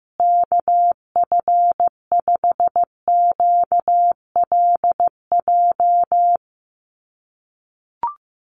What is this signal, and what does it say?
Morse code, heard as a keyed tone, about 15 wpm, 700 Hz. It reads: KF5QLJ
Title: USA Call Signs Repeated 3x 15wpm